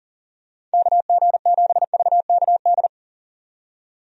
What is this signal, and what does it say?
Morse code, heard as a keyed tone, about 40 wpm, 700 Hz. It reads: KG7VKD
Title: USA Call Signs - 6 Characters - 40wpm